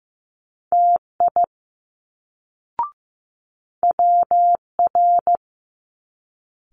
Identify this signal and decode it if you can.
Morse code, heard as a keyed tone, about 15 wpm, 700 Hz. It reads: TI WR
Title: Multi-letter Phonemes Quick Fire 15wpm